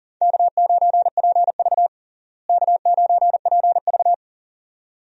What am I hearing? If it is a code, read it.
Morse code, heard as a keyed tone, about 40 wpm, 700 Hz. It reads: K9PV K9PV
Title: USA Call Signs - 4 Characters - Repeated 2x - 40wpm